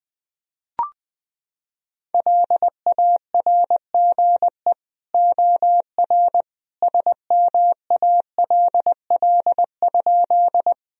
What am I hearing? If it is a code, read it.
Morse code, heard as a keyed tone, about 20 wpm, 700 Hz. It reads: LARGE OR SMALL?
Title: Sentences from Top 1000 Words - Review Part 1 - 20wpm